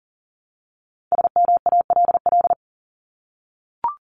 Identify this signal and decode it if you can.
Morse code, heard as a keyed tone, about 40 wpm, 700 Hz. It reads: SMALL
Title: Top 200 Words 40wpm